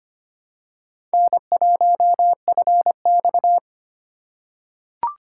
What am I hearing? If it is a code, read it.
Morse code, heard as a keyed tone, about 25 wpm, 700 Hz. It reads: N1FX
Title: USA Call Signs - 4 Characters - 3x Character Spacing - 25wpm